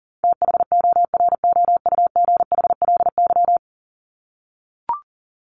Morse code, heard as a keyed tone, about 40 wpm, 700 Hz. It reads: THOROUGHLY
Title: LY Suffix - Top 100 Words with Base Words 40wpm